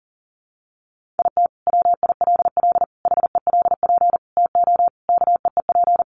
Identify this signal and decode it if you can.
Morse code, heard as a keyed tone, about 40 wpm, 700 Hz. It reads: IT WILL HELP TO KEEP
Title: Sentences from Top 1400 Words 40wpm